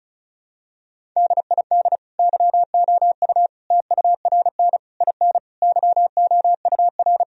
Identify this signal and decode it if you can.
Morse code, heard as a keyed tone, about 36 wpm, 700 Hz. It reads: DID YOU TURN IN YOUR
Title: Sentences from Top 1000 Words - Review Part 2 - 35wpm